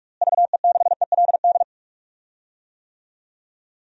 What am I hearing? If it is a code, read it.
Morse code, heard as a keyed tone, about 42 wpm, 700 Hz. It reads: VE6ELD
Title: DXCC-100 Call Signs - 6 Characters - Repeated 3x - 45wpm